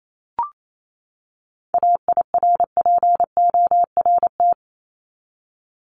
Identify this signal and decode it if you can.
Morse code, heard as a keyed tone, about 28 wpm, 700 Hz. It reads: AIRPORT
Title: Top 1751-2000 Words 28wpm